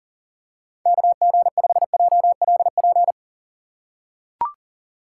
Morse code, heard as a keyed tone, about 37 wpm, 700 Hz. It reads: KG5JLP
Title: USA Call Signs - 6 Characters - 3x Character Spacing - 40wpm